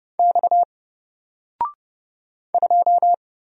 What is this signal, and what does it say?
Morse code, heard as a keyed tone, about 30 wpm, 700 Hz. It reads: X 2
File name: Single-Letter-Number-30wpm